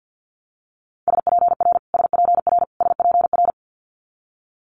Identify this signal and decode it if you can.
Morse code, heard as a keyed tone, about 50 wpm, 700 Hz. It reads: SPR SPR SPR
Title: Multi-letter Phonemes Repeated 3x 50wpm